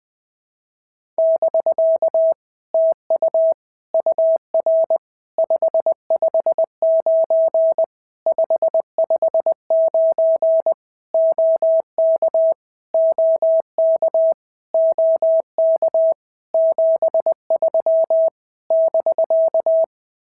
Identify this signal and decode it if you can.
Morse code, heard as a keyed tone, about 20 wpm, 650 Hz. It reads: <BK> TU UR 559 559 OK OK OK 73 <BK>
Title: POTA QSO Practice - 20wpm